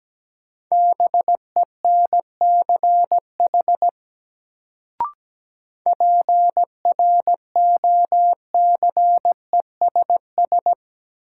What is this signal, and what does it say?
Morse code, heard as a keyed tone, about 17 wpm, 700 Hz. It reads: BENCH PROCESS
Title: Sentences from Top 1400 Words - New Word Drill Alt - 17wpm